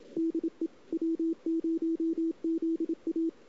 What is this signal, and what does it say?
Morse code, heard as a keyed tone, about 27 wpm, 330 Hz. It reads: DE W0ZA
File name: CW / 2008-ss-cw-k5tr-op-n5rz -02:00